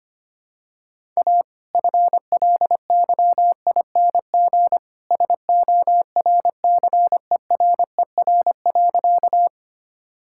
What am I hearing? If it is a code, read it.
Morse code, heard as a keyed tone, about 25 wpm, 700 Hz. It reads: A FLYING SORCERER.
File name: practice / 25wpm / All-Age Appropriate Jokes - 25wpm